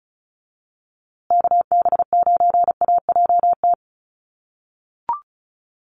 Morse code, heard as a keyed tone, about 34 wpm, 700 Hz. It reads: KB9AJT
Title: USA Call Signs Repeated 3x 35wpm